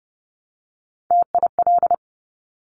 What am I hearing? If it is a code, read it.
Morse code, heard as a keyed tone, about 30 wpm, 690 Hz. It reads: TIL
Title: Top Three Letter Combinations in Words 30wpm